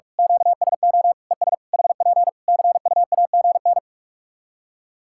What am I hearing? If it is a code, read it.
Morse code, heard as a keyed tone, about 41 wpm, 700 Hz. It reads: QSO ES HP CUAGN
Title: K1USN SST QSO Practice 45wpm